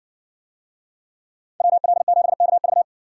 Morse code, heard as a keyed tone, about 60 wpm, 700 Hz. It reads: WL7KV